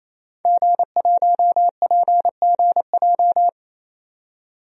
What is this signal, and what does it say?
Morse code, heard as a keyed tone, about 27 wpm, 700 Hz. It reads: G1PGJ
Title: DXCC-100 Call Signs - 5 Characters - 28wpm